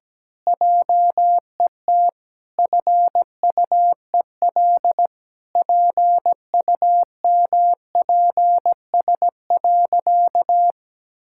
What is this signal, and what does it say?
Morse code, heard as a keyed tone, about 17 wpm, 700 Hz. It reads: JET FUEL PUMPS.
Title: Pangram Challenge v2 - 17wpm